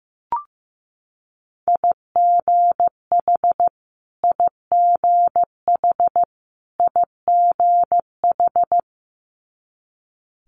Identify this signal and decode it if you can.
Morse code, heard as a keyed tone, about 15 wpm, 700 Hz. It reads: IGH IGH IGH
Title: Multi-letter Phonemes Repeated 3x 15wpm